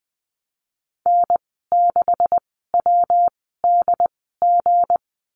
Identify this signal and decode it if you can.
Morse code, heard as a keyed tone, about 20 wpm, spaced as for 15 wpm, 700 Hz. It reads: N6WDG
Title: 238 Lesson 38g - Letter q - Call Signs 20wpm